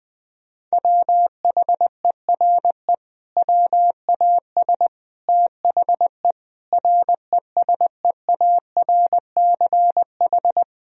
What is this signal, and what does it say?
Morse code, heard as a keyed tone, about 20 wpm, 700 Hz. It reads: WHERE WAS THE RESEARCH
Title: Sentences from Top 1000 Words - Review Part 4 - 20wpm